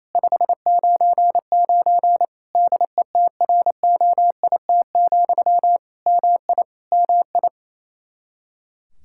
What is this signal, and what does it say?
Morse code, heard as a keyed tone, about 28 wpm, 700 Hz. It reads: 599 DETROIT, MI MI
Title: Practice QSO - Exchange 1 - Beginning - 28wpm